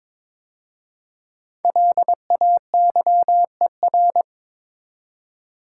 Morse code, heard as a keyed tone, about 22 wpm, 690 Hz. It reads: LAYER
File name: 5 Letter Words - First 3 Letters Given - 22wpm